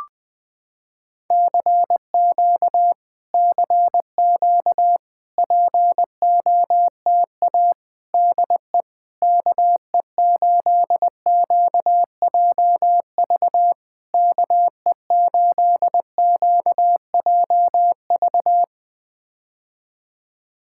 Morse code, heard as a keyed tone, about 20 wpm, 700 Hz. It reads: CQ CQ POTA DE KE8QJV KE8QJV
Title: POTA QSO Practice - 20wpm